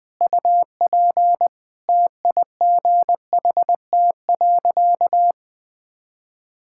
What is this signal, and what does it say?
Morse code, heard as a keyed tone, about 21 wpm, 700 Hz. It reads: UP TIGHT.
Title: Easy Sentences with Hints - 20wpm